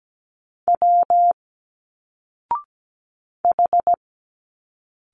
Morse code, heard as a keyed tone, about 17 wpm, 700 Hz. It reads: W H
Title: Single Letters 17wpm